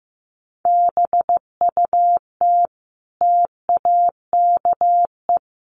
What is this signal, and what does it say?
Morse code, heard as a keyed tone, about 15 wpm, 700 Hz. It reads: BUT TAKE
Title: Sentences from Top 300 Words 15wpm